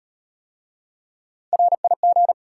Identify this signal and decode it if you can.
Morse code, heard as a keyed tone, about 33 wpm, 690 Hz. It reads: RIG